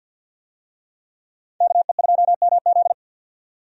Morse code, heard as a keyed tone, about 52 wpm, 700 Hz. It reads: KE2MZ
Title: USA Call Signs - 5 Characters - 50wpm